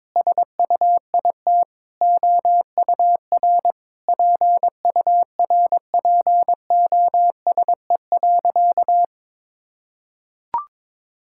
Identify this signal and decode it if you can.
Morse code, heard as a keyed tone, about 22 wpm, 700 Hz. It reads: SUIT OUR PURPOSE.
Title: Sentences from Top 1600 Words 22wpm